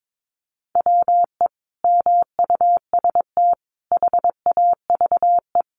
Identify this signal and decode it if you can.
Morse code, heard as a keyed tone, about 22 wpm, 700 Hz. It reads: WE MUST HAVE
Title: Sentences from Top 1300 Words 22wpm